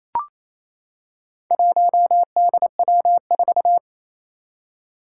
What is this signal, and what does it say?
Morse code, heard as a keyed tone, about 28 wpm, 700 Hz. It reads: 1DW4
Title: Four Challenging Letter-Number Combinations 28wpm